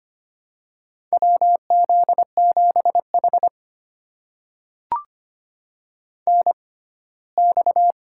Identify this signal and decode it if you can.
Morse code, heard as a keyed tone, about 25 wpm, spaced as for 17 wpm, 700 Hz. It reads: WZ7H NX
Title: USA Call Signs - 4 Characters - 6x Character Spacing - 25wpm